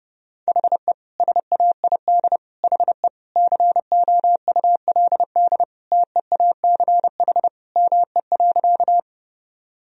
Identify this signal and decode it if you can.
Morse code, heard as a keyed tone, about 30 wpm, 700 Hz. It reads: HE SAID HE COULD TEACH ME.
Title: Sentences from Top 1000 Words - Review Part 2 - 30wpm